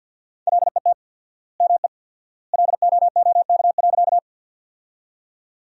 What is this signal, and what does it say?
Morse code, heard as a keyed tone, about 53 wpm, 700 Hz. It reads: LET ME LOOK.
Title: 4x Word-Spacing - Sentences from Top 300 Words - 50wpm